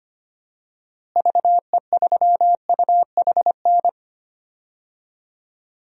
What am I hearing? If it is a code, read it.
Morse code, heard as a keyed tone, about 25 wpm, 700 Hz. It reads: VE3UHN
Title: DXCC-100 Call Signs - 6 Characters - Repeated 2x - 25wpm